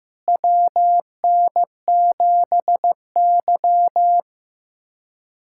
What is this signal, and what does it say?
Morse code, heard as a keyed tone, about 15 wpm, 700 Hz. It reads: WN7Y
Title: USA Callsign DE Callsign - 4 Characters - 15wpm